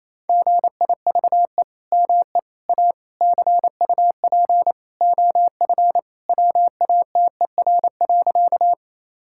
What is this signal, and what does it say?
Morse code, heard as a keyed tone, about 27 wpm, 700 Hz. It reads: GIVE ME A CUP OF WATER.
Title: Sentences from Top 1600 Words 28wpm